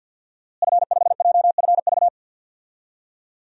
Morse code, heard as a keyed tone, about 53 wpm, 700 Hz. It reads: F5JFV
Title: DXCC-100 Call Signs - 5 Characters - Repeated 2x - 50wpm